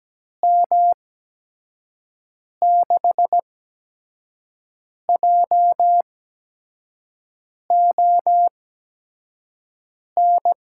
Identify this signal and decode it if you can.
Morse code, heard as a keyed tone, about 17 wpm, spaced as for 5 wpm, 700 Hz. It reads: M6JON
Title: USA Call Signs - 6 Characters - 8x Character Spacing - 17wpm